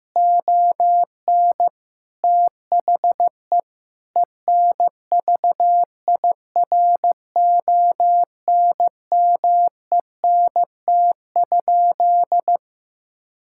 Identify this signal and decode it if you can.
Morse code, heard as a keyed tone, about 16 wpm, 700 Hz. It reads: ON THE ENVIRONMENT?
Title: Sentences from Top 1000 Words - Review Part 6 - 15wpm